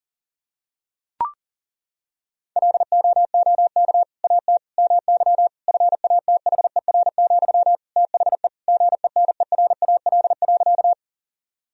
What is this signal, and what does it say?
Morse code, heard as a keyed tone, about 40 wpm, 700 Hz. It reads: LOOK AT MY FATHER, THE GENERAL.
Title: Sentences from Top 1000 Words - Review Part 3 - 40wpm